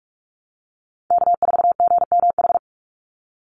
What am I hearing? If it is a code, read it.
Morse code, heard as a keyed tone, about 44 wpm, 700 Hz. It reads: K4GMH
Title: Call Signs Repeated 2x - 45wpm